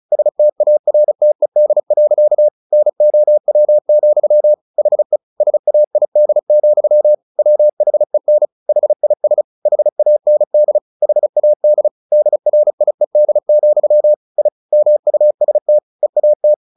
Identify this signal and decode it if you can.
Morse code, heard as a keyed tone, about 34 wpm, 600 Hz. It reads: STARTED. NOW, HE SAID, WHEN HIS HAND HAD DRIED, I MUST EAT
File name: the old man and the sea / the old man and the sea 1 35wpm